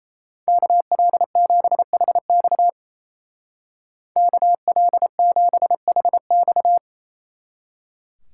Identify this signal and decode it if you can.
Morse code, heard as a keyed tone, about 35 wpm, 700 Hz. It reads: KL7HX KL7HX
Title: Speed Racing - USA 2x2 Call Signs 28wpm